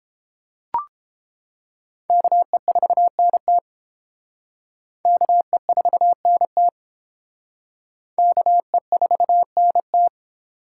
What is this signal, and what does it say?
Morse code, heard as a keyed tone, about 28 wpm, 700 Hz. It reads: KE4NT KE4NT KE4NT